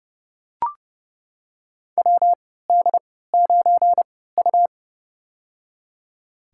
Morse code, heard as a keyed tone, about 30 wpm, spaced as for 17 wpm, 700 Hz. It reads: WD9U
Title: USA Call Signs - 4 Characters - 3x Character Spacing - 30wpm